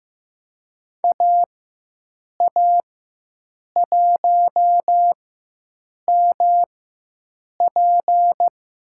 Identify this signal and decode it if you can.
Morse code, heard as a keyed tone, about 15 wpm, spaced as for 7 wpm, 700 Hz. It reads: AA1MP
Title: USA Call Signs - 5 Characters - 4x Character Spacing - 15wpm